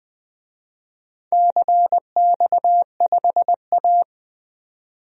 Morse code, heard as a keyed tone, about 21 wpm, 700 Hz. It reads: CX5A
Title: DXCC-100 Call Signs - 4 Characters - 20wpm